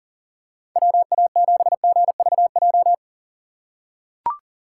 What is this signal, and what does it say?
Morse code, heard as a keyed tone, about 40 wpm, 700 Hz. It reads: WA7GVJ
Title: USA Call Signs - 6 Characters - 8x Character Spacing - 40wpm